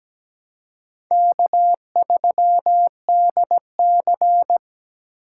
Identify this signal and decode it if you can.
Morse code, heard as a keyed tone, about 17 wpm, 700 Hz. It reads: K3DC
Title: USA Call Signs - 3x Character Spacing - 17wpm